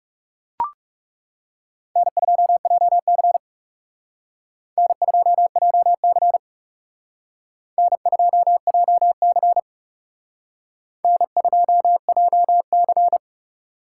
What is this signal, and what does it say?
Morse code, heard as a keyed tone, about 36 wpm, 700 Hz. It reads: N2JC N2JC N2JC N2JC
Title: Speed Racing - USA 1x2 Call Signs 30wpm